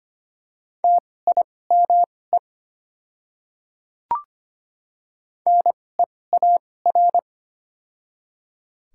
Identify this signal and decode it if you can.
Morse code, heard as a keyed tone, about 25 wpm, spaced as for 18 wpm, 700 Hz. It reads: TIME NEAR
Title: 2x Character Spacing - Top 500 Words - 25wpm